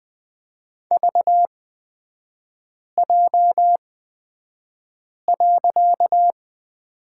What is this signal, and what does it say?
Morse code, heard as a keyed tone, about 20 wpm, spaced as for 5 wpm, 700 Hz. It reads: VJ.
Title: 255 Lesson 41e - Period - Three Characters 20wpm